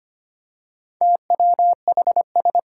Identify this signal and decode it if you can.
Morse code, heard as a keyed tone, about 25 wpm, 700 Hz. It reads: TWHS
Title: Four Challenging Letter-Number Combinations 25wpm